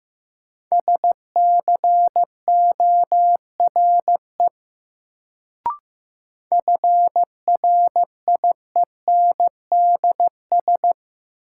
Morse code, heard as a keyed tone, about 15 wpm, 700 Hz. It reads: SCORE FRIENDS
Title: Sentences from Top 1600 Words - New Word Drill Alt - 15wpm